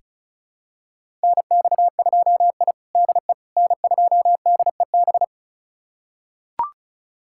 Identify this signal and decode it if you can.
Morse code, heard as a keyed tone, about 36 wpm, 700 Hz. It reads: NX2I DE N2DEB
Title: USA Callsign DE Callsign 35wpm